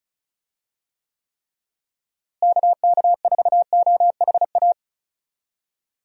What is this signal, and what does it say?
Morse code, heard as a keyed tone, about 36 wpm, 700 Hz. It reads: KK4OHA
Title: USA Call Signs - 6 Characters - Repeated 3x - 35wpm